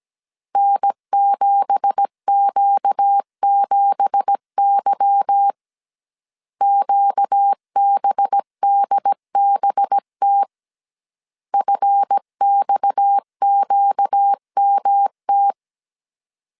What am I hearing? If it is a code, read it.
Morse code, heard as a keyed tone, about 17 wpm, 800 Hz. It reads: N7Q7Y QBDBT FXQMT
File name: CW / DET / 1 9 1500C S A 16wpm 2828